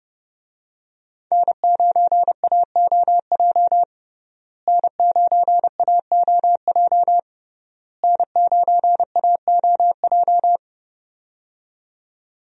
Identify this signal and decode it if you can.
Morse code, heard as a keyed tone, about 32 wpm, 700 Hz. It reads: N9AOJ N9AOJ N9AOJ
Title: USA Call Signs - 5 Characters - Repeated 3x - 30wpm